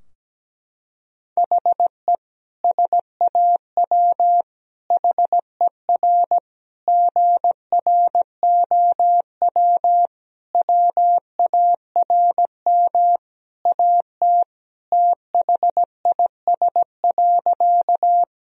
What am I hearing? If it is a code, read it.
Morse code, heard as a keyed tone, about 17 wpm, 700 Hz. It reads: HE SAW HER GROW WARM AT THIS.
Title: Sentences from Top 1000 Words - Review Part 2 - 17wpm